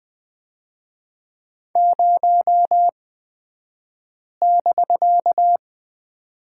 Morse code, heard as a keyed tone, about 20 wpm, spaced as for 5 wpm, 700 Hz. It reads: 0<BK>
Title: 249 Lesson 40d - BK - Two Characters 20wpm